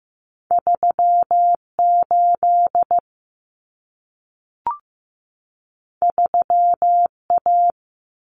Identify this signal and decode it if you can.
Morse code, heard as a keyed tone, about 15 wpm, 700 Hz. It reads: 38 3A
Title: Common Cut Numbers - 2-Digit 15wpm